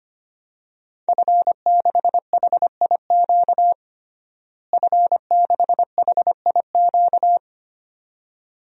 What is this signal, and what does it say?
Morse code, heard as a keyed tone, about 25 wpm, 700 Hz. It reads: F6HIQ F6HIQ
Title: DXCC-100 Call Signs - 5 Characters - Repeated 2x - 25wpm